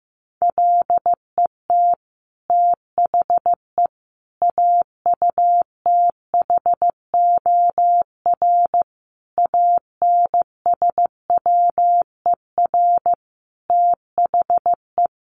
Morse code, heard as a keyed tone, about 15 wpm, 700 Hz. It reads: LET THE AUTHOR ANSWER THE
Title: Sentences from Top 1200 Words 15wpm